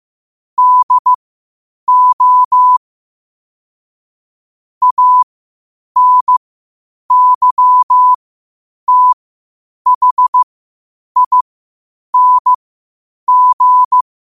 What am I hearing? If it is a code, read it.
Morse code, heard as a keyed tone, about 15 wpm, spaced as for 8 wpm, 1000 Hz. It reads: DO ANYTHING
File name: 8wpm